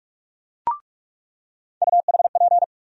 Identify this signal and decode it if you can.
Morse code, heard as a keyed tone, about 42 wpm, 700 Hz. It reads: UHP
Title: USA Call Sign Postfixes - 3 Letters 45wpm